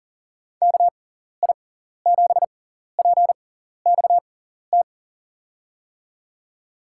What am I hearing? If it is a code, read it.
Morse code, heard as a keyed tone, about 40 wpm, spaced as for 14 wpm, 700 Hz. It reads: KI7PXT
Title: USA Call Signs - 6 Characters - 6x Character Spacing - 40wpm